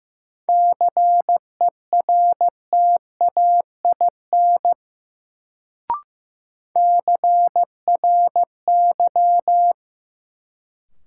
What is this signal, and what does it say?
Morse code, heard as a keyed tone, about 15 wpm, 700 Hz. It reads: CERTAIN CRY
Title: Sentences from Top 500 Words - New Word Drill - 15wpm